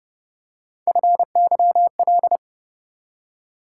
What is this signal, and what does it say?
Morse code, heard as a keyed tone, about 29 wpm, 700 Hz. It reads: FYL